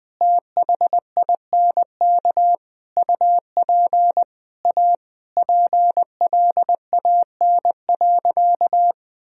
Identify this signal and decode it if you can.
Morse code, heard as a keyed tone, about 20 wpm, 700 Hz. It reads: THINK UP A PLAN.
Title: Sentences from Top 1000 Words - Review Part 2 - 20wpm